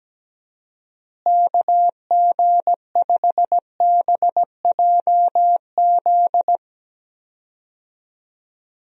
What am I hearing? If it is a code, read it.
Morse code, heard as a keyed tone, about 17 wpm, 700 Hz. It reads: KG5BJZ
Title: USA Call Signs - 6 Characters - Repeated 2x - 17wpm